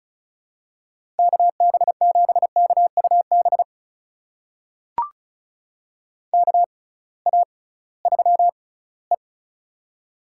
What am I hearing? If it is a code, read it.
Morse code, heard as a keyed tone, about 36 wpm, spaced as for 20 wpm, 700 Hz. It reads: KB7KUB KA3E
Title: USA Call Signs - 6x Character Spacing - 35wpm